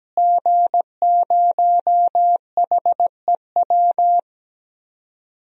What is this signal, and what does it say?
Morse code, heard as a keyed tone, about 17 wpm, 700 Hz. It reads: G0HEW
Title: DXCC-100 Call Signs - 5 Characters - Repeated 3x - 17wpm